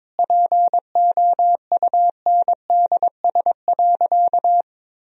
Morse code, heard as a keyed tone, about 22 wpm, 700 Hz. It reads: POUNDS.
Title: Sentences from Top 1000 Words - Review Part 4 - 22wpm